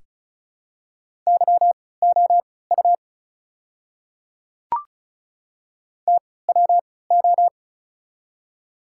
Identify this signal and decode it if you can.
Morse code, heard as a keyed tone, about 36 wpm, spaced as for 20 wpm, 690 Hz. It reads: YOU TWO
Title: 3x Character Spacing - Top 100 Words - 35wpm